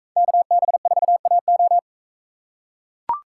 Morse code, heard as a keyed tone, about 41 wpm, 700 Hz. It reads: KB4AO